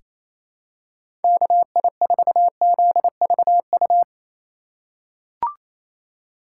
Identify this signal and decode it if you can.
Morse code, heard as a keyed tone, about 28 wpm, 700 Hz. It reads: KI4ZVU
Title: K1USN SST QSO Practice - 28wpm